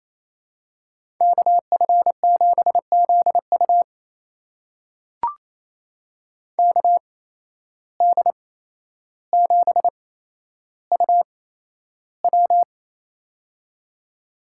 Morse code, heard as a keyed tone, about 28 wpm, spaced as for 12 wpm, 700 Hz. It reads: KF7ZU KD7UW
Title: USA Call Signs - 5 Characters - 8x Character Spacing - 28wpm